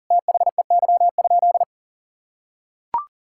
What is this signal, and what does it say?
Morse code, heard as a keyed tone, about 40 wpm, 700 Hz. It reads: THEY?
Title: Sentences from Top 1000 Words - Review Part 5 - 40wpm